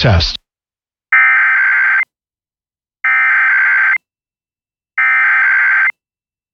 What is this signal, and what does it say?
Emergency Alert System (SAME) tones (header bursts), heard as harsh buzzing bursts, with speech. It is heard from 1.1 to 5.9 s.